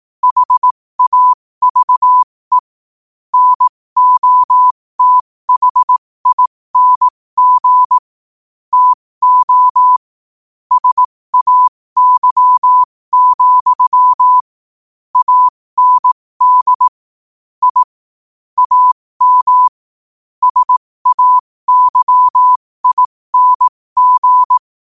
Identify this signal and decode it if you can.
Morse code, heard as a keyed tone, about 18 wpm, spaced as for 15 wpm, 1000 Hz. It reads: HAVE NOTHING TO SAY, AND I AM SAYING
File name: qotdfiles / 15wpm